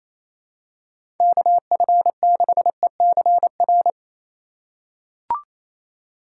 Morse code, heard as a keyed tone, about 28 wpm, 700 Hz. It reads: KF6ECR
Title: USA Call Signs - 4x Character Spacing - 28wpm